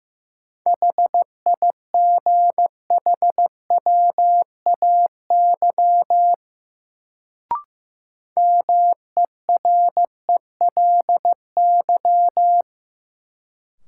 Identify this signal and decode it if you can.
Morse code, heard as a keyed tone, about 15 wpm, 700 Hz. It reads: HIGHWAY MERELY
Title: Sentences from Top 1400 Words - New Word Drill - 15wpm